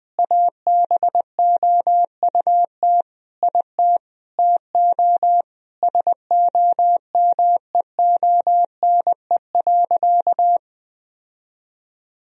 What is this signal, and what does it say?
Morse code, heard as a keyed tone, about 20 wpm, 700 Hz. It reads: ABOUT IT TO SOMEONE.
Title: Sentences from Top 1000 Words - Review Part 4 - 20wpm